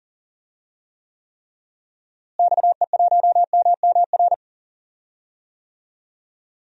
Morse code, heard as a keyed tone, about 38 wpm, 700 Hz. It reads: XE1MMR
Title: DXCC-100 Call Signs - 6 Characters - Repeated 3x - 40wpm